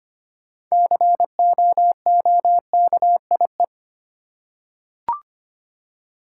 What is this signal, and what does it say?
Morse code, heard as a keyed tone, about 25 wpm, 690 Hz. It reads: COOKIE
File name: practice / 25wpm / Easy Words with Hints - 25wpm